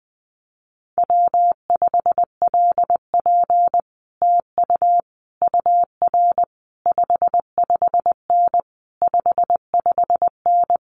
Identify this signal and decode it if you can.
Morse code, heard as a keyed tone, about 20 wpm, 700 Hz. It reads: W5LP TU UR 55N 55N
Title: Common QSO Phrases Encore 20wpm